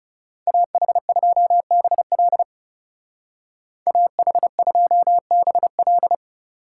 Morse code, heard as a keyed tone, about 40 wpm, 700 Hz. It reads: AH2BL AH2BL